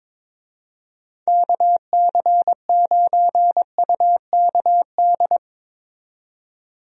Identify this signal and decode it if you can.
Morse code, heard as a keyed tone, about 22 wpm, 700 Hz. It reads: KC9UKD
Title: USA Call Signs - 6 Characters - 22wpm